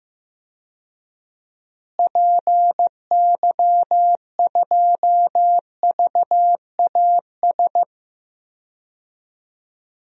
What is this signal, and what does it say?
Morse code, heard as a keyed tone, about 15 wpm, 700 Hz. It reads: PY2VAS
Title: DXCC-100 Call Signs - 6 Characters - Repeated 2x - 15wpm